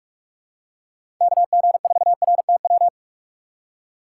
Morse code, heard as a keyed tone, about 46 wpm, 700 Hz. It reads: KG4RTW
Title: USA Call Signs - 6 Characters - 3x Character Spacing - 45wpm